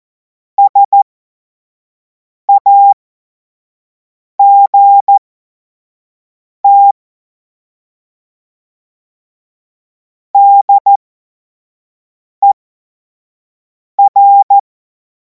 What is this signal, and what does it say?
Morse code, heard as a keyed tone, about 14 wpm, spaced as for 5 wpm, 800 Hz. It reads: SAGT DER
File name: Morsekurs / Witze / Witz 31. März 2026 25 F70